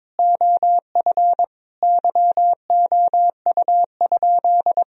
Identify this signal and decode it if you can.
Morse code, heard as a keyed tone, about 22 wpm, 700 Hz. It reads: OF YOU?
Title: Sentences from Top 1000 Words - Review Part 1 - 22wpm